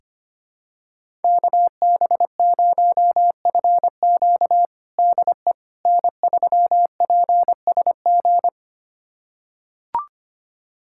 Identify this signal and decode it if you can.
Morse code, heard as a keyed tone, about 25 wpm, 700 Hz. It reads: KB0FQ DE N3PSG
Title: USA Callsign DE Callsign - 5 Characters - 25wpm